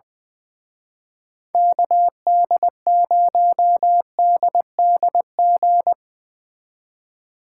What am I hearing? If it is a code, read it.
Morse code, heard as a keyed tone, about 22 wpm, 700 Hz. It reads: KD0DDG
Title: Speed Racing - USA 2x3 Call Signs 20wpm